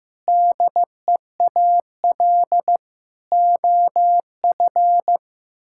Morse code, heard as a keyed tone, about 15 wpm, 700 Hz. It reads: DEAL OF
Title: Sentences from Top 1000 Words - Review Part 6 - 15wpm